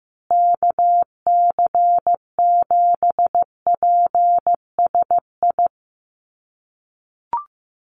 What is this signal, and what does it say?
Morse code, heard as a keyed tone, about 15 wpm, 700 Hz. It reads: KC7PSI
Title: USA Call Signs Repeated 3x 15wpm